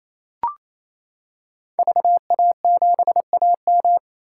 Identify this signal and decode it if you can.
Morse code, heard as a keyed tone, about 28 wpm, 700 Hz. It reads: VA7AM
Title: K1USN SST QSO Practice - 28wpm